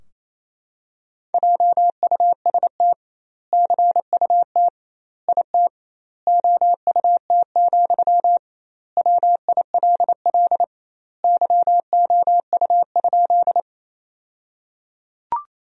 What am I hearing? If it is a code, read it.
Morse code, heard as a keyed tone, about 28 wpm, 700 Hz. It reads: JUST CUT IT OUT, WILL YOU?
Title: 2x Word-Spacing - Sentences from Top 300 Words - 28wpm